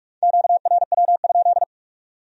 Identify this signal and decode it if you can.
Morse code, heard as a keyed tone, about 39 wpm, 700 Hz. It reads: QRW?